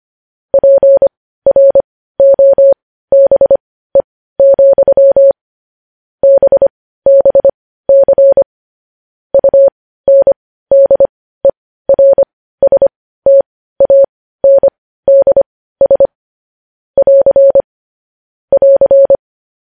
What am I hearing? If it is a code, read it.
Morse code, heard as a keyed tone, about 25 wpm, spaced as for 15 wpm, 550 Hz. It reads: PROBE, BBC UNDERSTANDS <AR> <AR>
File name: podcast / News headlines-1x-25x15